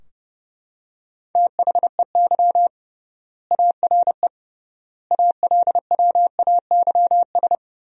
Morse code, heard as a keyed tone, about 32 wpm, 700 Hz. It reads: THEY ARE ALWAYS
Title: 3x Word-Spacing - Sentences from Top 300 Words - 30wpm